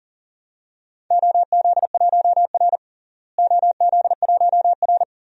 Morse code, heard as a keyed tone, about 40 wpm, 700 Hz. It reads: OZ1R OZ1R
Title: DXCC-100 Call Signs - 4 Characters - Repeated 3x - 40wpm